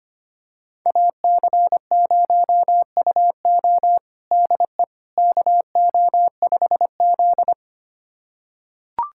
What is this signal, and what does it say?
Morse code, heard as a keyed tone, about 25 wpm, 700 Hz. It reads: AC0UO DE KO5Z
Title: USA Callsign DE Callsign 25wpm